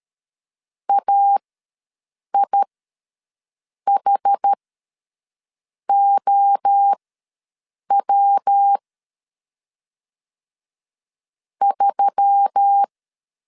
Morse code, heard as a keyed tone, about 13 wpm, spaced as for 6 wpm, 800 Hz. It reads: AIHOW 3
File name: LEZIONI / 5 LBVC / 5 LBV ws 12wpm 3015